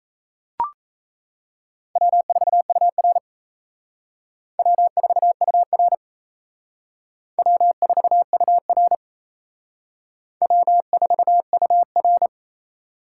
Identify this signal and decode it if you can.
Morse code, heard as a keyed tone, about 36 wpm, 700 Hz. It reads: W4UR W4UR W4UR W4UR